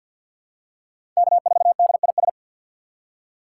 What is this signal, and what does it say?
Morse code, heard as a keyed tone, about 48 wpm, 700 Hz. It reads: K4DIS
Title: USA Call Signs - 5 Characters - 3x Character Spacing - 50wpm